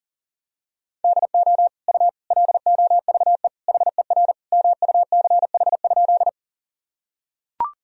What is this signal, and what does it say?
Morse code, heard as a keyed tone, about 37 wpm, 700 Hz. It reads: DO U LOVE HER MUCH?
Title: Sentences from Top 400 Words and Abbreviations 40wpm